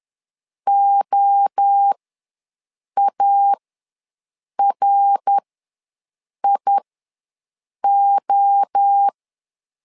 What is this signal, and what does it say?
Morse code, heard as a keyed tone, about 10 wpm, spaced as for 6 wpm, 800 Hz. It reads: OARIO
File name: LEZIONI / 3 MWRO / 3 MWRO ws 10wpm 2310